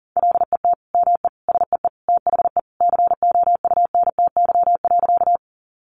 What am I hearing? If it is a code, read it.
Morse code, heard as a keyed tone, about 40 wpm, 700 Hz. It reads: LET ME SEE THE COUNTY.
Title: Sentences from Top 1200 Words 40wpm